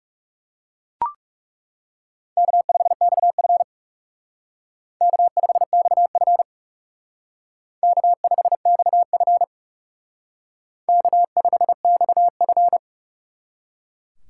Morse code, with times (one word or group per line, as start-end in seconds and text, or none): 2.4-3.6 K5XF
5.0-6.4 K5XF
7.8-9.4 K5XF
10.9-12.8 K5XF